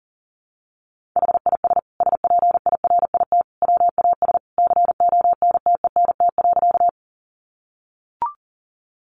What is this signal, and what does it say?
Morse code, heard as a keyed tone, about 40 wpm, 700 Hz. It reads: HIS SPIRIT WAS CONTENT.
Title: Sentences from Top 1400 Words 40wpm